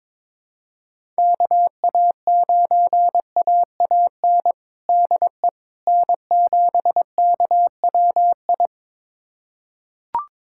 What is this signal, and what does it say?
Morse code, heard as a keyed tone, about 22 wpm, 700 Hz. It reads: KA9AAN DE N7KWI
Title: USA Callsign DE Callsign 22wpm